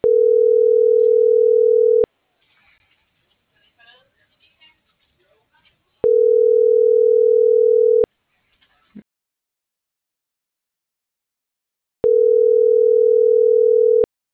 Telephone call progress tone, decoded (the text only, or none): ringback tone